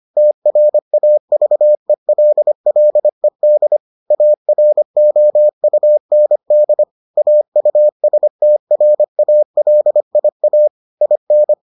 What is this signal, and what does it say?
Morse code, heard as a keyed tone, about 25 wpm, 600 Hz. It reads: TRAVELLED AROUND AUSTRALIA IN